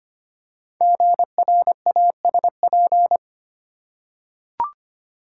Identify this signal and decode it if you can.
Morse code, heard as a keyed tone, about 25 wpm, 700 Hz. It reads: GRASP
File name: 5 Letter Words - First 3 Letters Given - 25wpm